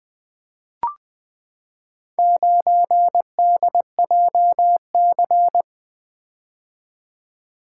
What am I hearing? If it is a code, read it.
Morse code, heard as a keyed tone, about 20 wpm, 700 Hz. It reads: 9DJC
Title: Four Letter-Number Combinations - 20wpm